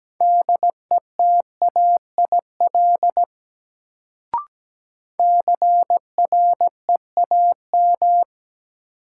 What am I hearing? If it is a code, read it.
Morse code, heard as a keyed tone, about 17 wpm, 700 Hz. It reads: DETAIL CREAM
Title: Sentences from Top 1500 Words - New Word Drill Alt - 17wpm